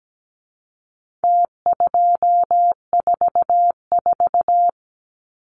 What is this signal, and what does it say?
Morse code, heard as a keyed tone, about 17 wpm, 700 Hz. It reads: T244
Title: Numbers - Contest Style - 4-Digit 17wpm